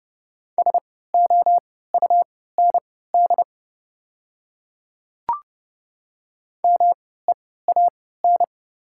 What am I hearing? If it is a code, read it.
Morse code, heard as a keyed tone, about 30 wpm, spaced as for 17 wpm, 700 Hz. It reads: SOUND MEAN
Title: 3x Character Spacing - Top 300 Words - 30wpm